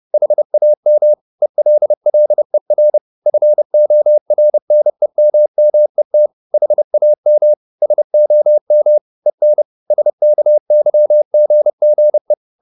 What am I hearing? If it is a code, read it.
Morse code, heard as a keyed tone, about 30 wpm, 600 Hz. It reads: HAM ELLER FORNEMMET HAM SOM EN SKYGGE
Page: Ferie 30 wpm